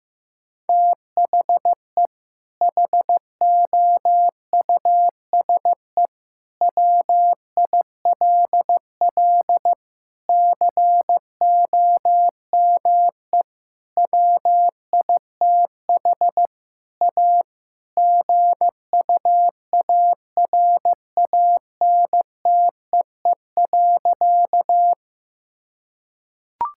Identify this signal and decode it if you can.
Morse code, heard as a keyed tone, about 15 wpm, 700 Hz. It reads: THE HOUSE WILL COME WITH A GUARANTEE.
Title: Sentences from Top 1600 Words 15wpm